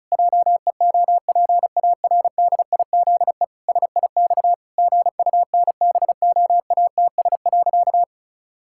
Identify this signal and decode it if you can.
Morse code, heard as a keyed tone, about 34 wpm, 700 Hz. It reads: JEOPARDIZE SIX GUNBOATS.
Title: Pangram Challenge v2 - 35wpm